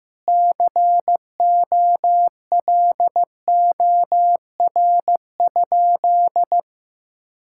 Morse code, heard as a keyed tone, about 15 wpm, 700 Hz. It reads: COLOR?
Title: Sentences from Top 1000 Words - Review Part 6 - 15wpm